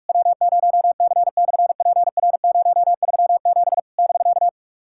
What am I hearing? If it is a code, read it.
Morse code, heard as a keyed tone, about 46 wpm, 690 Hz. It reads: W0C/PR037 <BK>
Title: SOTA QSO Practice 45wpm